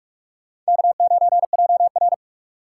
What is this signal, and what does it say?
Morse code, heard as a keyed tone, about 46 wpm, 700 Hz. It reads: K9JR
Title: USA Call Signs - 4 Characters - 2x Character Spacing - 45wpm